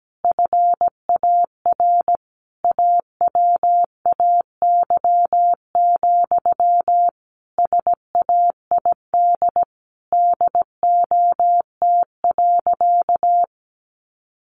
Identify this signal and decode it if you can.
Morse code, heard as a keyed tone, about 17 wpm, 700 Hz. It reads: FAR AWAY, SAID DOT.
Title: Sentences from Top 1500 Words 17wpm